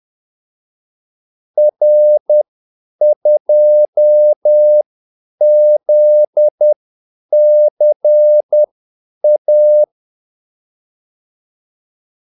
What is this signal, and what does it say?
Morse code, heard as a keyed tone, about 10 wpm, spaced as for 8 wpm, 600 Hz. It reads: R2ZCA
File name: Callsigns-10-8wpm